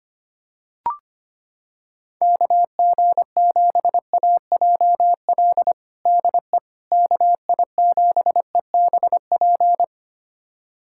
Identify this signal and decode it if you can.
Morse code, heard as a keyed tone, about 25 wpm, 700 Hz. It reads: KG7AJL DE KI7EBP
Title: USA Callsign DE Callsign - 6 Characters - 25wpm